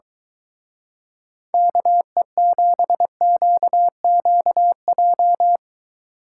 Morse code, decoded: KE7QQJ

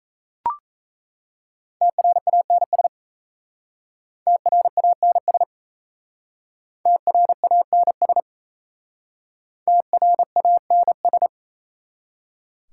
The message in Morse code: TRANS TRANS TRANS TRANS

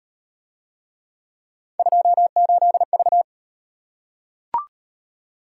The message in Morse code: 28V

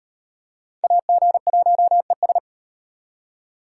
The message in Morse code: AG1ES

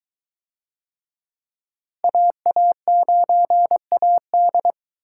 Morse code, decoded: AA9AD

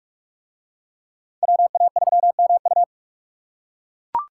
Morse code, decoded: WA3MU

